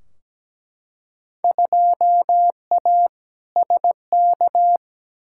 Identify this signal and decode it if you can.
Morse code, heard as a keyed tone, about 17 wpm, 700 Hz. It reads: 2A SK
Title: Field Day Exchanges Encore - 17wpm